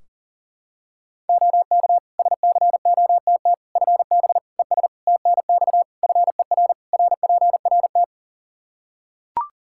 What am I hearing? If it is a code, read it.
Morse code, heard as a keyed tone, about 40 wpm, 700 Hz. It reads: OK SCOTT FB ES TNX FER RPRT
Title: Practice QSO - Exchange 2 - Info - 40wpm